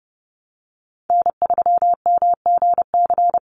Morse code, heard as a keyed tone, about 30 wpm, 700 Hz. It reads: N3MGC
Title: USA Call Signs 30wpm